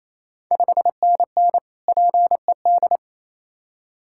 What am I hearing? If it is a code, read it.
Morse code, heard as a keyed tone, about 28 wpm, 690 Hz. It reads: 5NN PED
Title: Ontario QSO Party Exchange 28wpm